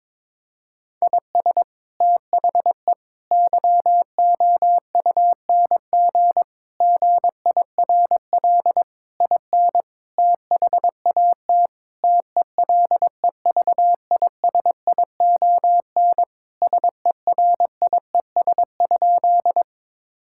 Morse code, heard as a keyed tone, about 22 wpm, 700 Hz. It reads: IS THE YOUNG GIRL IN THAT TELEVISION SERIES?
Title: Sentences from Top 1000 Words - Review Part 5 - 22wpm